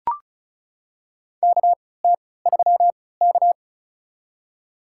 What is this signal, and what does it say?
Morse code, heard as a keyed tone, about 36 wpm, spaced as for 20 wpm, 700 Hz. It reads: KT3K
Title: USA Call Signs - 4 Characters - 3x Character Spacing - 35wpm